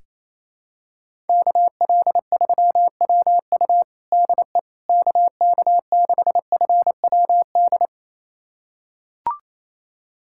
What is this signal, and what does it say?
Morse code, heard as a keyed tone, about 27 wpm, 700 Hz. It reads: KL3WU DE KK6FWD
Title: USA Callsign DE Callsign 28wpm